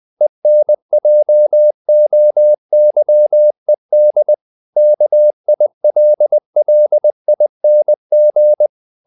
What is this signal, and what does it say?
Morse code, heard as a keyed tone, about 20 wpm, 600 Hz. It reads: ENJOYED KILLING
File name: the old man and the sea part 2 20wpm